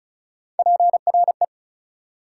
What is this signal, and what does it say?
Morse code, heard as a keyed tone, about 34 wpm, 700 Hz. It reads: PRE
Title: Prefixes and Suffixes 35wpm